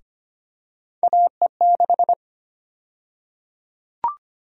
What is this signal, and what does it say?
Morse code, heard as a keyed tone, about 25 wpm, 700 Hz. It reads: AE6
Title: USA Call Sign Prefixes - Repeated 2x - 25wpm